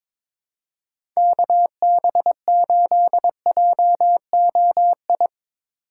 Morse code, heard as a keyed tone, about 21 wpm, 700 Hz. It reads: KB8JOI
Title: USA Call Signs - 6 Characters - 4x Character Spacing - 22wpm